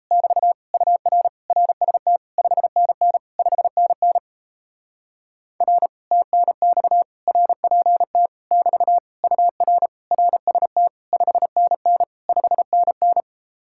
Morse code, heard as a keyed tone, about 37 wpm, 700 Hz. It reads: <BT> UR RST 5NN 5NN R TNX RPT <BT> UR RST 5NN 5NN